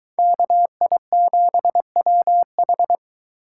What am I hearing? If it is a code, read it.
Morse code, heard as a keyed tone, about 22 wpm, 700 Hz. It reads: KI7WH